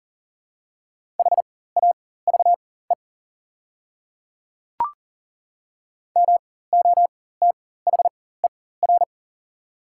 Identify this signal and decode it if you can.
Morse code, heard as a keyed tone, about 40 wpm, spaced as for 19 wpm, 690 Hz. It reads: HAVE MOTHER
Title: 4x Character Spacing - Top 500 Words - 40wpm